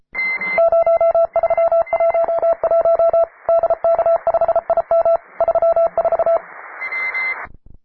Morse code, heard as a keyed tone, about 33 wpm, 670 Hz, with radio noise. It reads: 0311 DK5IM 34
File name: wae / cw advanced / p3f cw